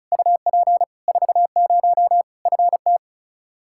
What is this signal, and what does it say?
Morse code, heard as a keyed tone, about 36 wpm, 700 Hz. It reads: UP 40 FT
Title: Practice QSO - Exchange 2 - Info - 35wpm